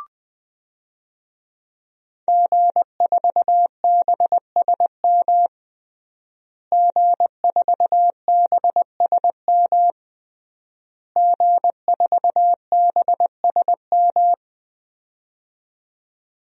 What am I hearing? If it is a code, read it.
Morse code, heard as a keyed tone, about 20 wpm, 700 Hz. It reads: G4BSM G4BSM G4BSM
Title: DXCC-100 Call Signs - 5 Characters - Repeated 3x - 20wpm